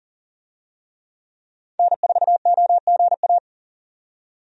Morse code, heard as a keyed tone, about 40 wpm, 700 Hz. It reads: N4OGA
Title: USA Call Signs - 5 Characters - 40wpm